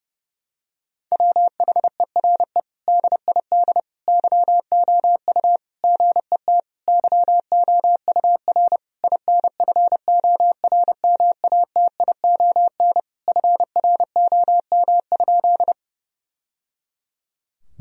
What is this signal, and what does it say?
Morse code, heard as a keyed tone, about 32 wpm, 700 Hz. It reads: WHERE DID YOU GET YOUR INFORMATION FROM?
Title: Sentences from Top 1000 Words - Review Part 3 - 30wpm